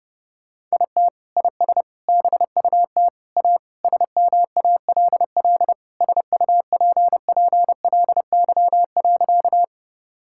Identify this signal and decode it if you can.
Morse code, heard as a keyed tone, about 32 wpm, 700 Hz. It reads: IT IS BUT A SMALL SUPPLY.
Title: Sentences from Top 1000 Words - Review Part 5 - 30wpm